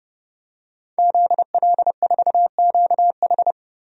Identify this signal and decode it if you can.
Morse code, heard as a keyed tone, about 32 wpm, 700 Hz. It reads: ZL4QH
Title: DXCC-100 Call Signs - 5 Characters - 30wpm